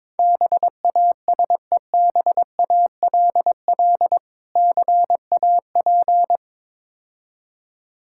Morse code, heard as a keyed tone, about 22 wpm, 700 Hz. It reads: BASEBALL CAP
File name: Difficult Words with Hints - 22wpm